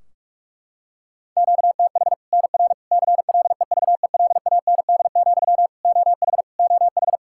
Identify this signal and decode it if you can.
Morse code, heard as a keyed tone, about 46 wpm, 700 Hz. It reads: QTH NR CLEVELAND, OH OH
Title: Practice QSO - Exchange 1 - Beginning - 45wpm